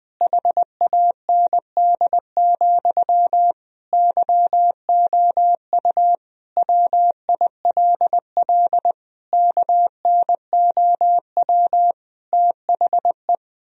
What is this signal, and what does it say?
Morse code, heard as a keyed tone, about 20 wpm, 700 Hz. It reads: HAND, YOU WILL KNOW THE
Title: Sentences from Top 1000 Words - Review Part 3 - 20wpm